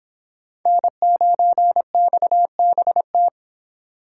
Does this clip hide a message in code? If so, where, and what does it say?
Yes: Morse code from 0.7 to 3.3 s, heard as a keyed tone, about 26 wpm, 700 Hz. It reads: N9XBT